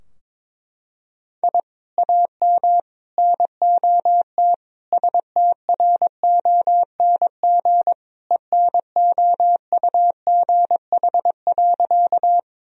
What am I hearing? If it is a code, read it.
Morse code, heard as a keyed tone, about 22 wpm, 700 Hz. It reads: I AM NOT STRONG ENOUGH.
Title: Sentences from Top 1000 Words - Review Part 3 - 22wpm